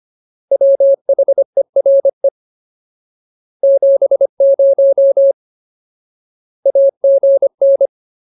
Morse code, heard as a keyed tone, about 25 wpm, 550 Hz. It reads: WHERE 70 AGN
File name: Ultimate Word List / 25WPM Ultimate